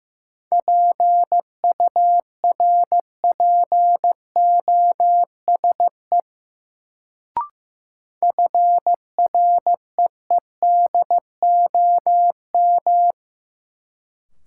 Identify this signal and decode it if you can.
Morse code, heard as a keyed tone, about 15 wpm, 700 Hz. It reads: PURPOSE FREEDOM
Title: Sentences from Top 800 Words - New Word Drill - 15wpm